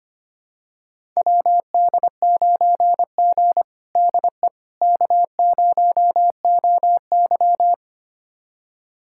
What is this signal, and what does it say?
Morse code, heard as a keyed tone, about 25 wpm, 700 Hz. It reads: WD9G DE K0OY
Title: USA Callsign DE Callsign - 4 Characters - 25wpm